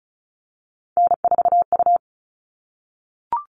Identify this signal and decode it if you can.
Morse code, heard as a keyed tone, about 34 wpm, 700 Hz. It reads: N4U
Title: All Cut Numbers - 3-Digit 35wpm